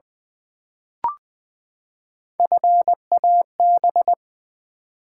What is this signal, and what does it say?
Morse code, heard as a keyed tone, about 20 wpm, 700 Hz. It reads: FAB
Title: Build A Word - 3 Letters At A Time - 8-Letter Words - 20wpm